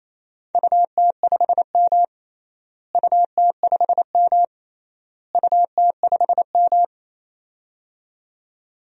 Morse code, heard as a keyed tone, about 28 wpm, 700 Hz. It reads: UT5M UT5M UT5M
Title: DXCC-100 Call Signs - 4 Characters - Repeated 3x - 28wpm